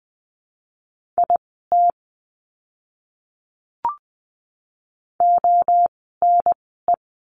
Morse code, heard as a keyed tone, about 20 wpm, spaced as for 14 wpm, 700 Hz. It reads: IT ONE
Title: 108 Lesson 18f - Letter m - Words 20wpm